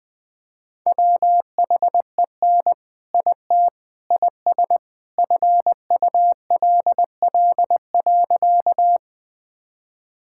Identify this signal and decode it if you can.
Morse code, heard as a keyed tone, about 21 wpm, 700 Hz. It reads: WHEN IT IS FULL.
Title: All-Age Appropriate Jokes - 20wpm